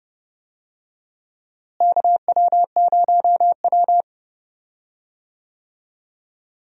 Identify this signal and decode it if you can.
Morse code, heard as a keyed tone, about 30 wpm, 700 Hz. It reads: KW0W
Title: USA Call Signs - 4 Characters - Repeated 3x - 30wpm